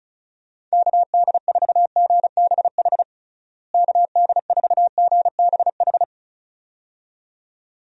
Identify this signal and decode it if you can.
Morse code, heard as a keyed tone, about 36 wpm, 700 Hz. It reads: KD4GBH KD4GBH
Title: USA Call Signs - 6 Characters - Repeated 2x - 35wpm